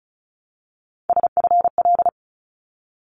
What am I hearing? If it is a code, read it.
Morse code, heard as a keyed tone, about 34 wpm, 700 Hz. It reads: SFL
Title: ARRL Sections 35wpm